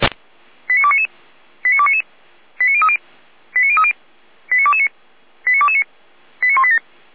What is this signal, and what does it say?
Five-tone selective call (ZVEI), heard as quick runs of five tones, tones of about 70 ms. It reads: 89299 89299 89030 89030 89229 89229 89187